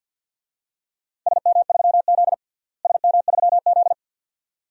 Morse code, heard as a keyed tone, about 52 wpm, 700 Hz. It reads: SM3Z SM3Z
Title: DXCC-100 Call Signs - 4 Characters - Repeated 2x - 50wpm